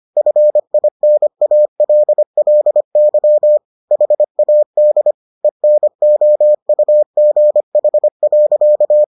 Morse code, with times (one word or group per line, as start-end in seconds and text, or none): 0.2-3.6 FINALLY
3.9-5.1 HAD
5.4-9.0 ENOUGH.